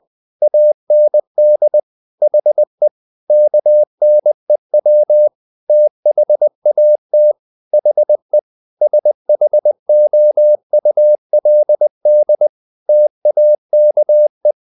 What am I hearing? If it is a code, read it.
Morse code, heard as a keyed tone, about 20 wpm, 600 Hz. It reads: AND HE KNEW THAT HE SHOULD TAKE
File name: the old man and the sea / the old man and the sea part 1 20wpm